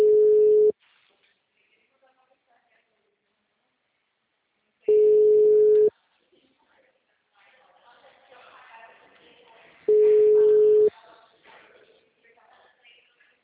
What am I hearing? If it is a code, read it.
Ringback tone (Europe, 425 Hz), with speech.
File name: out- -2046- - - .16